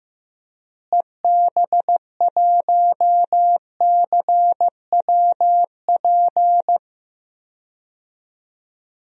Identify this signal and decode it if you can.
Morse code, heard as a keyed tone, about 15 wpm, 700 Hz. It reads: EB1CWP
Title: DXCC-100 Call Signs - 6 Characters - Repeated 3x - 15wpm